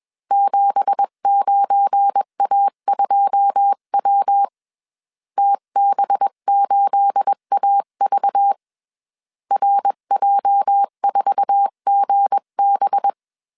Morse code, heard as a keyed tone, about 21 wpm, 800 Hz. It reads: 79A2W TB8AV RJ4GB
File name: CW / DET / 1 9 1500C S B 20wpm 2313